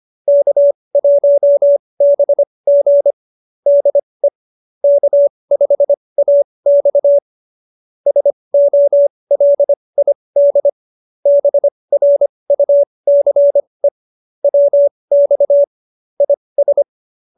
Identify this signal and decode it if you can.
Morse code, heard as a keyed tone, about 25 wpm, spaced as for 20 wpm, 570 Hz. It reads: K1BG DE K5AX SOLID BRUCE WX IS